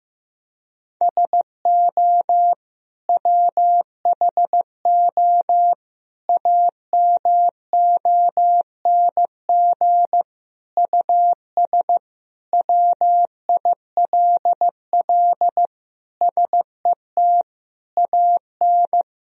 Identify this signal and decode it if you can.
Morse code, heard as a keyed tone, about 15 wpm, 700 Hz. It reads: SO WHO AMONG US WILL SET AN
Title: Sentences from Top 1000 Words - Review Part 3 - 15wpm